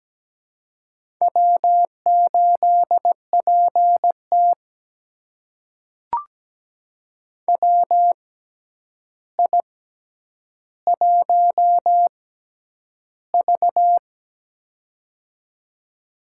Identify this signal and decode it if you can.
Morse code, heard as a keyed tone, about 17 wpm, spaced as for 9 wpm, 700 Hz. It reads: W8PT WI1V
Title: USA Call Signs - 4 Characters - 6x Character Spacing - 17wpm